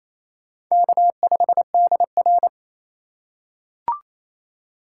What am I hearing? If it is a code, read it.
Morse code, heard as a keyed tone, about 28 wpm, 700 Hz. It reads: K5DR
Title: USA Call Signs - 6x Character Spacing - 28wpm